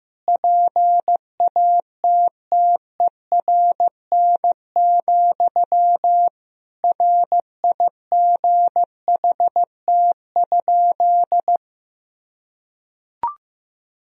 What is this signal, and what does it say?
Morse code, heard as a keyed tone, about 15 wpm, 700 Hz. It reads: PATTERN, RIGHT?
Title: Sentences from Top 1000 Words - Review Part 2 - 15wpm